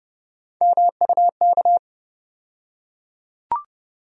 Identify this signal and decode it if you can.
Morse code, heard as a keyed tone, about 31 wpm, 700 Hz. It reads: MUK
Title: USA Call Sign Postfixes - 3 Letters 30wpm